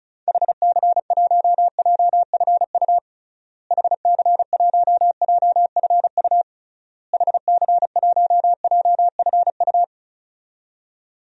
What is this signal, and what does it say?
Morse code, heard as a keyed tone, about 36 wpm, 700 Hz. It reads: HC1JFU HC1JFU HC1JFU
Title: DXCC-100 Call Signs - 6 Characters - Repeated 3x - 35wpm